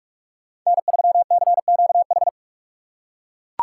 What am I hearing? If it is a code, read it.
Morse code, heard as a keyed tone, about 41 wpm, 700 Hz. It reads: N3CQH
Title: USA Call Signs - 5 Characters - 4x Character Spacing - 45wpm